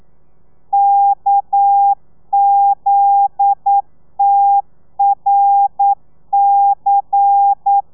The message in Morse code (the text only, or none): KZTRC